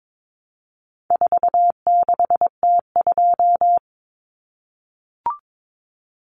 Morse code, heard as a keyed tone, about 22 wpm, 700 Hz. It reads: 46T2
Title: Common Cut Numbers - 4-Digit 22wpm